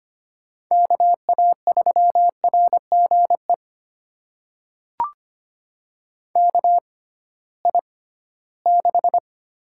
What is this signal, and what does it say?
Morse code, heard as a keyed tone, about 25 wpm, spaced as for 16 wpm, 700 Hz. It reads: KA3RGE KI6
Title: USA Call Signs - 6 Characters - 6x Character Spacing - 25wpm